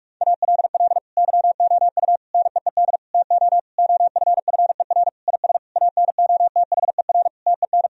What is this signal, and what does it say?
Morse code, heard as a keyed tone, about 46 wpm, 700 Hz. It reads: ALL YOU NEED TO OFFER IS ANOTHER TEN
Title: Sentences from Top 1000 Words - Review Part 4 - 45wpm